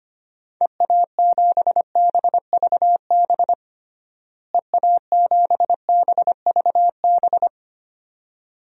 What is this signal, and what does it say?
Morse code, heard as a keyed tone, about 25 wpm, 700 Hz. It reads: EA7BVB EA7BVB
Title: DXCC-100 Call Signs - 6 Characters - Repeated 2x - 25wpm